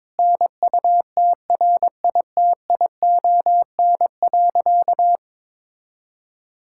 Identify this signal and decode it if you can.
Morse code, heard as a keyed tone, about 22 wpm, 700 Hz. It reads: NUTRITION.
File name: Difficult Sentences with Hints - 22wpm